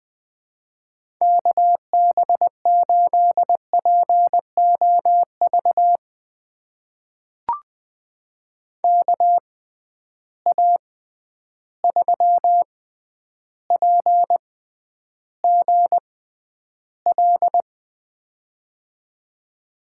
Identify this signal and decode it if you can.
Morse code, heard as a keyed tone, about 20 wpm, spaced as for 10 wpm, 700 Hz. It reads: KB8POV KA3PGL
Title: USA Call Signs - 6 Characters - 6x Character Spacing - 20wpm